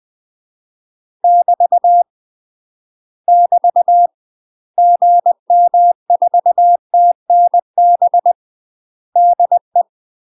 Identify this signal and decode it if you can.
Morse code, heard as a keyed tone, about 20 wpm, 700 Hz. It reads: <BT> <BT> GM4TNB DE
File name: UK MASTER QSOs TRAINING / 20WPM / UK MASTER QSOs 0062